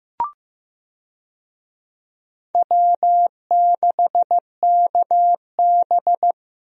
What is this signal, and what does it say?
Morse code, heard as a keyed tone, about 15 wpm, 700 Hz. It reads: W6KB
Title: USA Call Signs - 4 Characters - Repeated 3x - 15wpm